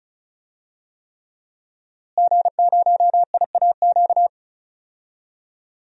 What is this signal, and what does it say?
Morse code, heard as a keyed tone, about 36 wpm, 700 Hz. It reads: G0IAQ
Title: DXCC-100 Call Signs - 5 Characters - Repeated 3x - 35wpm